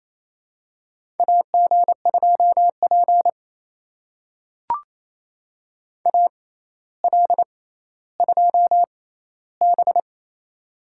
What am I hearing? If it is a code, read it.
Morse code, heard as a keyed tone, about 28 wpm, spaced as for 14 wpm, 700 Hz. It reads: AG2P AL2B
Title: USA Call Signs - 4 Characters - 6x Character Spacing - 28wpm